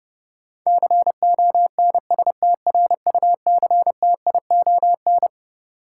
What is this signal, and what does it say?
Morse code, heard as a keyed tone, about 32 wpm, 700 Hz. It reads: CONSTRUCTION
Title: Sentences from Top 1600 Words - New Word Drill - 30wpm